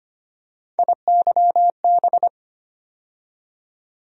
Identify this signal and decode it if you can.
Morse code, heard as a keyed tone, about 25 wpm, 700 Hz. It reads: IYB
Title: USA Call Sign Postfixes - 3 Letters 25wpm